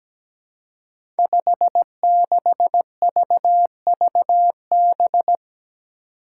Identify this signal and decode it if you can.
Morse code, heard as a keyed tone, about 17 wpm, 700 Hz. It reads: 56VVB
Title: Five Challenging Letter-Number Combinations 17wpm